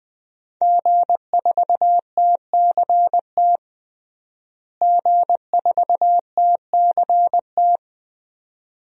Morse code, heard as a keyed tone, about 20 wpm, 700 Hz. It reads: G4TCT G4TCT
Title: DXCC-100 Call Signs - 5 Characters - Repeated 2x - 20wpm